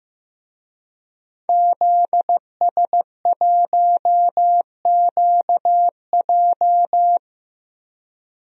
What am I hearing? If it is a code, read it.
Morse code, heard as a keyed tone, about 15 wpm, 700 Hz. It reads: ZS1QJ
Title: DXCC-100 Call Signs - 5 Characters - 15wpm